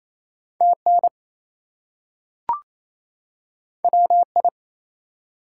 Morse code, heard as a keyed tone, about 28 wpm, 700 Hz. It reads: TN WI
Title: US States and Canadian Provinces 28wpm